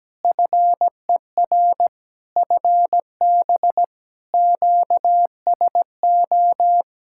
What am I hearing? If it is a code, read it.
Morse code, heard as a keyed tone, about 17 wpm, 700 Hz. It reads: FER FB QSO
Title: K1USN SST QSO Practice 17wpm